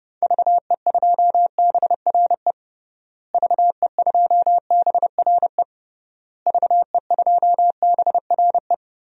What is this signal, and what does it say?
Morse code, heard as a keyed tone, about 32 wpm, 700 Hz. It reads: VE2BRE VE2BRE VE2BRE
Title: DXCC-100 Call Signs - 6 Characters - Repeated 3x - 30wpm